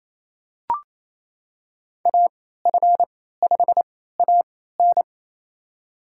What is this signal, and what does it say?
Morse code, heard as a keyed tone, about 28 wpm, spaced as for 16 wpm, 690 Hz. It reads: AF5AN
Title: USA Call Signs - 3x Character Spacing - 28wpm